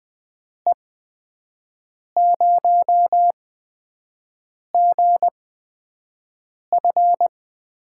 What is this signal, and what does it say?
Morse code, heard as a keyed tone, about 20 wpm, spaced as for 5 wpm, 700 Hz. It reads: E0GF
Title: USA Call Signs - 6 Characters - 8x Character Spacing - 20wpm